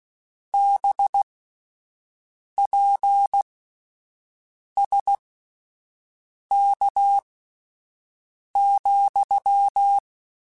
Morse code, heard as a keyed tone, about 16 wpm, spaced as for 5 wpm, 800 Hz. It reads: BPSK,
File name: QSOs Spaced